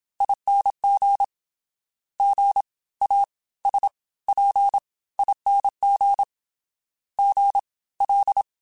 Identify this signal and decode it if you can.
Morse code, heard as a keyed tone, about 27 wpm, spaced as for 19 wpm, 800 Hz. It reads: ING GASPING GL